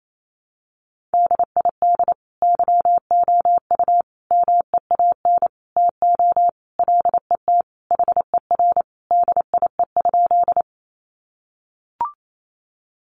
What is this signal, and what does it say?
Morse code, heard as a keyed tone, about 28 wpm, 700 Hz. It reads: DID YOU MEAN TO LET HER DIE?
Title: Sentences from Top 1300 Words 28wpm